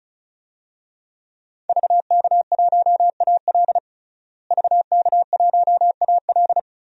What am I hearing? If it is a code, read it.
Morse code, heard as a keyed tone, about 34 wpm, 700 Hz. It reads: VK1AL VK1AL
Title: DXCC-100 Call Signs - 5 Characters - Repeated 3x - 35wpm